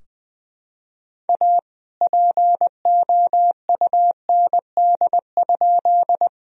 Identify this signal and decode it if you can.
Morse code, heard as a keyed tone, about 20 wpm, 700 Hz. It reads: A POUND?
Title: Sentences from Top 1000 Words - Review Part 2 - 20wpm